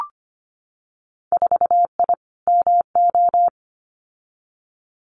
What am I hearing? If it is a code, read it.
Morse code, heard as a keyed tone, about 25 wpm, 690 Hz. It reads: 4I MO
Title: Winter Field Day Exchanges 25wpm